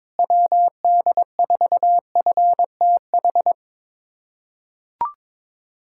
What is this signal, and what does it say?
Morse code, heard as a keyed tone, about 21 wpm, 700 Hz. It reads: WD4FTH
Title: USA Callsign DE Callsign - 6 Characters - 22wpm